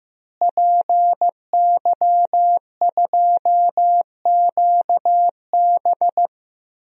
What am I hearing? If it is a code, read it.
Morse code, heard as a keyed tone, about 16 wpm, 700 Hz. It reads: PY2QB
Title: DXCC-100 Call Signs - 5 Characters - Repeated 2x - 15wpm